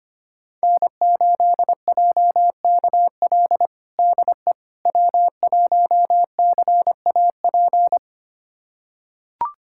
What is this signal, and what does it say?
Morse code, heard as a keyed tone, about 25 wpm, 700 Hz. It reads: N8JKL DE W1CAP
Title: USA Callsign DE Callsign - 5 Characters - 25wpm